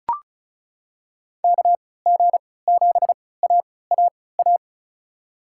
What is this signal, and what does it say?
Morse code, heard as a keyed tone, about 34 wpm, spaced as for 20 wpm, 690 Hz. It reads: KG7AAA
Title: USA Call Signs - 3x Character Spacing - 35wpm